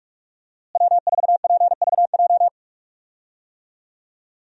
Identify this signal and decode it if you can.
Morse code, heard as a keyed tone, about 46 wpm, 700 Hz. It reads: W4PVJ
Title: USA Call Signs - 5 Characters - Repeated 2x - 45wpm